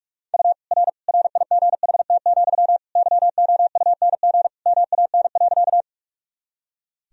Morse code, heard as a keyed tone, about 43 wpm, 700 Hz. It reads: U R RIGHT, YOUNG MAN.
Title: Sentences from Top 400 Words and Abbreviations 45wpm